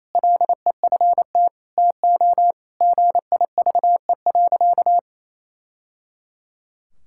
Morse code, heard as a keyed tone, about 27 wpm, 700 Hz. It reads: LEFT TO GIVE.
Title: Sentences from Top 1000 Words - Review Part 6 - 28wpm